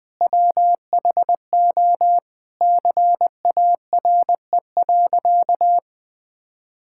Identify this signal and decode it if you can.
Morse code, heard as a keyed tone, about 20 wpm, 700 Hz. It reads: WHO CARE.
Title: Sentences with 9-Word Phrases - 20wpm